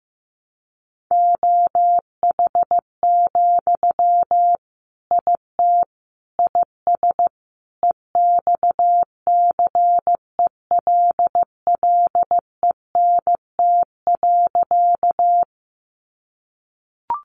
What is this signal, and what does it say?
Morse code, heard as a keyed tone, about 15 wpm, 700 Hz. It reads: OH, IT IS EXCELLENT.
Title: Sentences from Top 1000 Words 15wpm